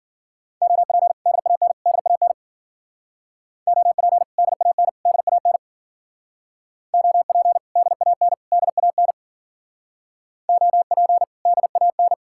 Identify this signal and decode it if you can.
Morse code, heard as a keyed tone, about 54 wpm, 700 Hz. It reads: OP DAN DAN OP DAN DAN OP DAN DAN OP DAN